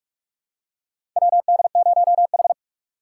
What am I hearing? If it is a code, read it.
Morse code, heard as a keyed tone, about 44 wpm, 700 Hz. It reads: WD0H
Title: USA Call Signs - 4 Characters - 4x Character Spacing - 45wpm